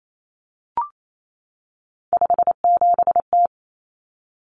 Morse code, heard as a keyed tone, about 28 wpm, 700 Hz. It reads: 57T
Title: Common Cut Numbers - 3-Digit 28wpm